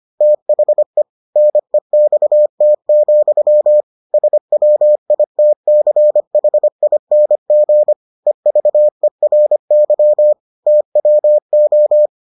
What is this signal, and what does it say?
Morse code, heard as a keyed tone, about 25 wpm, 600 Hz. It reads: THE NEXT, SWITCHING EVERY TWO